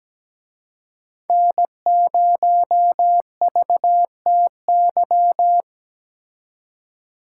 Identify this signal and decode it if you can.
Morse code, heard as a keyed tone, about 21 wpm, 700 Hz. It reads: N0VTY
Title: Speed Racing - USA Call Signs 17wpm